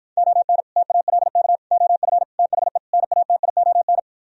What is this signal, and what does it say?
Morse code, heard as a keyed tone, about 52 wpm, 700 Hz. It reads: ON TALK OF THE NATION